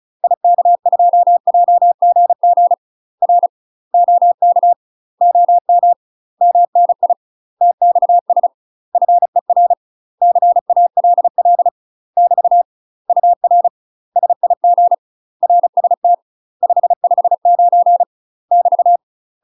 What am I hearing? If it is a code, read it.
Morse code, heard as a keyed tone, about 34 wpm, 700 Hz. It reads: IK2JGG R OK OM MNI TXS FER CALL <BT> UR SIG RST 559 <BT>
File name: UK MASTER QSOs TRAINING / 35WPM / UK MASTER QSOs 0001